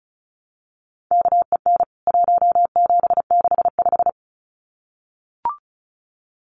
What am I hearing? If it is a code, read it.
Morse code, heard as a keyed tone, about 34 wpm, 700 Hz. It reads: KEN 1765
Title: CWT Exchange 35wpm